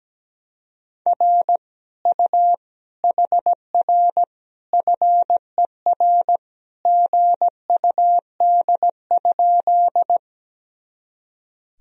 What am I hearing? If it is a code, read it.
Morse code, heard as a keyed tone, about 17 wpm, 700 Hz. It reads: R U HR FER GUD?
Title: Sentences from Top 200 Words and Abbreviations 17wpm